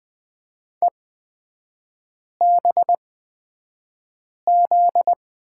Morse code, heard as a keyed tone, about 20 wpm, spaced as for 5 wpm, 700 Hz. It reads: EBZ
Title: 243 Lesson 39e - Letter z - Three Characters 20wpm